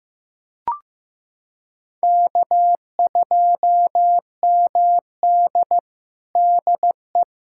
Morse code, heard as a keyed tone, about 16 wpm, 700 Hz. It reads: K2MD DE
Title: USA Callsign DE Callsign - 4 Characters - 15wpm